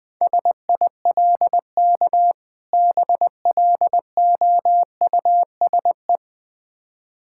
Morse code, heard as a keyed tone, about 21 wpm, 690 Hz. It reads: SILK BLOUSE
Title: Difficult Words with Hints - 20wpm